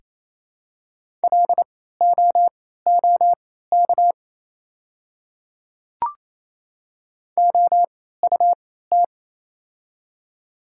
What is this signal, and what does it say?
Morse code, heard as a keyed tone, about 28 wpm, spaced as for 16 wpm, 700 Hz. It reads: LOOK OUT
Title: 3x Character Spacing - Top 200 Words - 28wpm